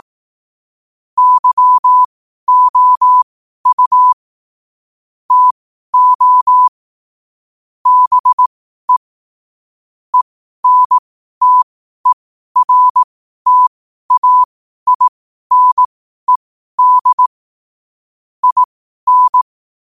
Morse code, heard as a keyed tone, about 18 wpm, spaced as for 12 wpm, 1000 Hz. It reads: YOU TO BE ENTERTAINED IN
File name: qotdfiles / 12wpm